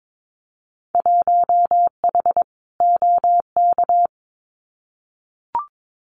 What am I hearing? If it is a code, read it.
Morse code, heard as a keyed tone, about 22 wpm, 700 Hz. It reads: 1H OK
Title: Winter Field Day Exchanges 22wpm